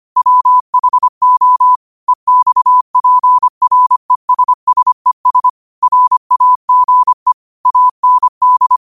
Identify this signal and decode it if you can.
Morse code, heard as a keyed tone, about 25 wpm, 1000 Hz. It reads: WHO EXPRESSES RAGE AND